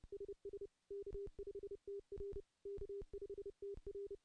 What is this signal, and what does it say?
Morse code, heard as a keyed tone, about 30 wpm, 400 Hz. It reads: SS K5TR K5TR
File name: contests / CW-SS-k5tr- -0600